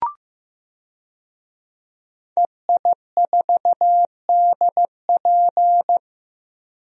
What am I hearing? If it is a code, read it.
Morse code, heard as a keyed tone, about 15 wpm, 700 Hz. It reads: EI4DP
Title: DXCC-100 Call Signs - 5 Characters - Repeated 2x - 15wpm